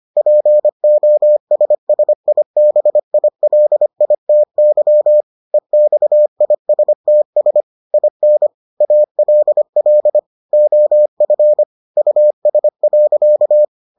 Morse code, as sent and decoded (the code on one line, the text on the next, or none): .--. --- ... ... .. -... .. .-.. .. - -.-- / . -..- .. ... - ... / .. -. / .- .-.. .-.. / --- ..-. / ..- ... .-.-.-
POSSIBILITY EXISTS IN ALL OF US.